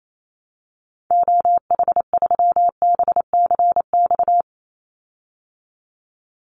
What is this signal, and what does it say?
Morse code, heard as a keyed tone, about 28 wpm, 700 Hz. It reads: OH3BCX
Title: Call Signs Repeated 3x - 28wpm